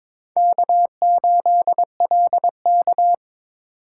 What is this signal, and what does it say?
Morse code, heard as a keyed tone, about 22 wpm, 700 Hz. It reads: K8LK
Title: USA Call Signs - 4 Characters - 4x Character Spacing - 22wpm